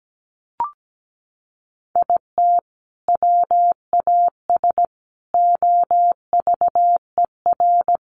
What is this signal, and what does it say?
Morse code, heard as a keyed tone, about 17 wpm, 700 Hz. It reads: IT WAS OVER
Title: Sentences from Top 600 Words 17wpm